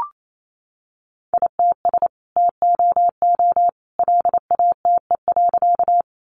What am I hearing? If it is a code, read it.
Morse code, heard as a keyed tone, about 28 wpm, 700 Hz. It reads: ITS TOO LATE.
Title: Sentences from Top 600 Words 28wpm